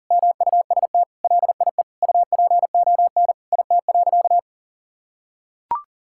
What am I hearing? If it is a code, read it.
Morse code, heard as a keyed tone, about 40 wpm, 700 Hz. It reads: MUST LIE UPON IT.
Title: Sentences from Top 1600 Words 40wpm